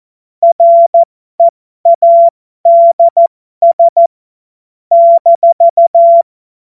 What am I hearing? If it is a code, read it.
Morse code, heard as a keyed tone, about 14 wpm, spaced as for 12 wpm, 680 Hz. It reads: READS -
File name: Morsekurs / Witze